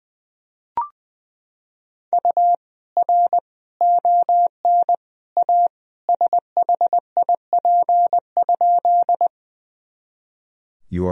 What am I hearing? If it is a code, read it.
Morse code, heard as a keyed tone, about 20 wpm, 700 Hz. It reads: U R ON A SHIP?
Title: Sentences from Top 400 Words and Abbreviations 20wpm